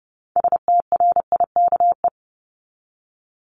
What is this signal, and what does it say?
Morse code, heard as a keyed tone, about 30 wpm, 700 Hz. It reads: STRIKE
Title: Top 1501-1750 Words 30wpm